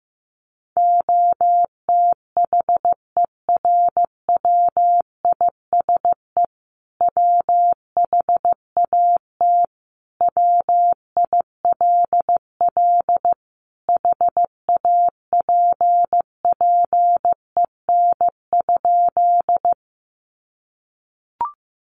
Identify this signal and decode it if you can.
Morse code, heard as a keyed tone, about 15 wpm, 700 Hz. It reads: OTHERWISE WHAT WILL HAPPEN?
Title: Sentences from Top 1200 Words 15wpm